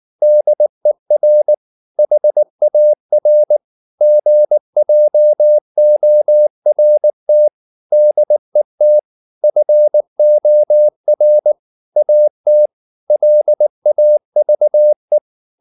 Morse code, heard as a keyed tone, about 19 wpm, 600 Hz. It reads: DER HAR GJORT DET FOR AT LAVE
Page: Ferie 19 wpm